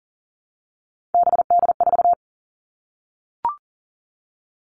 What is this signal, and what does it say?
Morse code, heard as a keyed tone, about 40 wpm, 700 Hz. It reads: BD4
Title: Callsign Prefixes for Top 100 DXCC - 40wpm